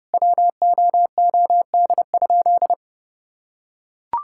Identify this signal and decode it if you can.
Morse code, heard as a keyed tone, about 30 wpm, 700 Hz. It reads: WOOD?
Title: Sentences from Top 1600 Words 30wpm